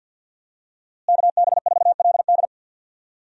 Morse code, heard as a keyed tone, about 51 wpm, 700 Hz. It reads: KB4LD
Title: USA Call Signs - 5 Characters - Repeated 2x - 50wpm